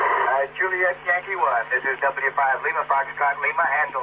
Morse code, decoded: none